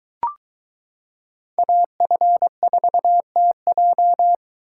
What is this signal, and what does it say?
Morse code, heard as a keyed tone, about 22 wpm, 700 Hz. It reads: AF4TJ